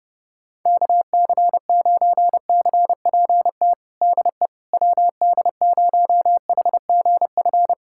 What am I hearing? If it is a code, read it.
Morse code, heard as a keyed tone, about 32 wpm, 700 Hz. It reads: KC9CPT DE WD0HGF
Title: USA Callsign DE Callsign - 6 Characters - 30wpm